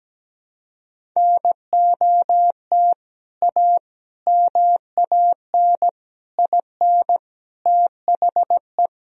Morse code, heard as a keyed tone, about 17 wpm, 700 Hz. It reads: NOT A MAN IN THE
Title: Sentences from Top 1000 Words - Review Part 1 - 17wpm